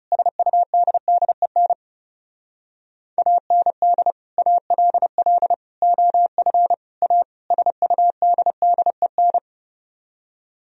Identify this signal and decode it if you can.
Morse code, heard as a keyed tone, about 38 wpm, 700 Hz. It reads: SUDDEN AND ALL OF A SUDDEN